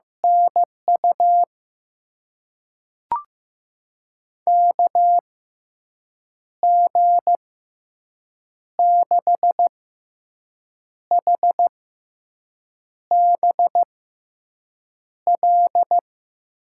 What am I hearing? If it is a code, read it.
Morse code, heard as a keyed tone, about 15 wpm, spaced as for 6 wpm, 700 Hz. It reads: NU KG6HBL
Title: USA Call Signs - 6 Characters - 6x Character Spacing - 15wpm